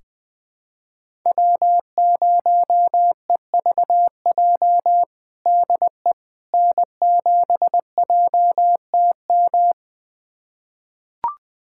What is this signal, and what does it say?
Morse code, heard as a keyed tone, about 20 wpm, 700 Hz. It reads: W0EVJ DE N7JTM
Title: USA Callsign DE Callsign - 5 Characters - 20wpm